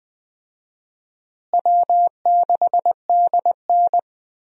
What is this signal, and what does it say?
Morse code, heard as a keyed tone, about 20 wpm, 700 Hz. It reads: W6DN
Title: USA Call Signs - 4 Characters - Repeated 2x - 20wpm